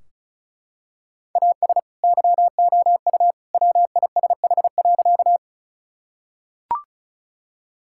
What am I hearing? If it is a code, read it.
Morse code, heard as a keyed tone, about 36 wpm, 700 Hz. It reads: AS YOU WISH.
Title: Sentences from Top 1600 Words 35wpm